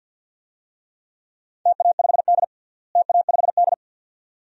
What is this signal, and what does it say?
Morse code, heard as a keyed tone, about 49 wpm, 700 Hz. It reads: TA5D TA5D
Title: DXCC-100 Call Signs - 4 Characters - Repeated 2x - 50wpm